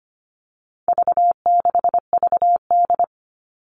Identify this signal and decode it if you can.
Morse code, heard as a keyed tone, about 25 wpm, 700 Hz. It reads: V6VD
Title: All Cut Numbers - 4-Digit 25wpm